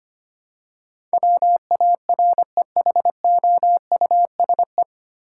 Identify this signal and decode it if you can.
Morse code, heard as a keyed tone, about 25 wpm, 700 Hz. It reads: WAREHOUSE
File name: Nine Letter Words - 25wpm